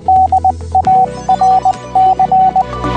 Morse code, heard as a keyed tone, about 20 wpm, 760 Hz. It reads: DARC
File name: radio-darc / DARC-Radio- -64K